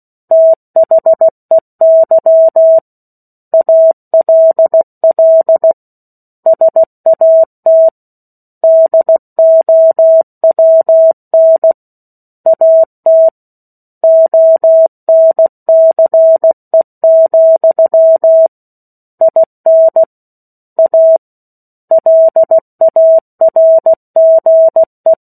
Morse code, heard as a keyed tone, about 16 wpm, 650 Hz. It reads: THEY ALL SAT DOWN AT ONCE, IN A LARGE
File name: 16 WPM